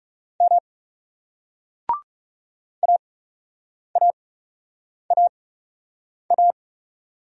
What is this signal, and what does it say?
Morse code, heard as a keyed tone, about 40 wpm, spaced as for 8 wpm, 700 Hz. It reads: M AAAA